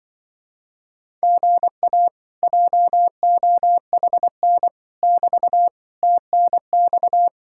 Morse code, heard as a keyed tone, about 25 wpm, 700 Hz. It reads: GA JOHN <BT> TNX